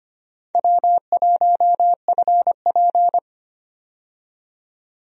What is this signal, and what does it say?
Morse code, heard as a keyed tone, about 25 wpm, 700 Hz. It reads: W1FP
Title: USA Call Signs - 4 Characters - 25wpm